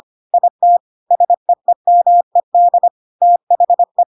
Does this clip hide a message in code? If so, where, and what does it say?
Yes: Morse code from 0.3 to 4.0 s, heard as a keyed tone, about 25 wpm, 700 Hz. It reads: IT SEEMED THE